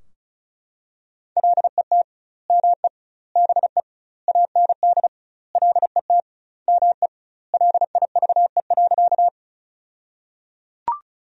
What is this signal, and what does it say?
Morse code, heard as a keyed tone, about 34 wpm, 700 Hz. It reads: LET ME BE AND LET ME LIVE.
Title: 2x Word-Spacing - Sentences from Top 300 Words - 35wpm